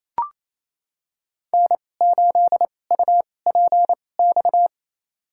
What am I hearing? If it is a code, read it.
Morse code, heard as a keyed tone, about 28 wpm, spaced as for 20 wpm, 700 Hz. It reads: N8UPX
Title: USA Call Signs - 5 Characters - 2x Character Spacing - 28wpm